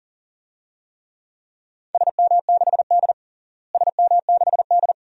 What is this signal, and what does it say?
Morse code, heard as a keyed tone, about 40 wpm, 700 Hz. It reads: SM6D SM6D
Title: DXCC-100 Call Signs - 4 Characters - Repeated 2x - 40wpm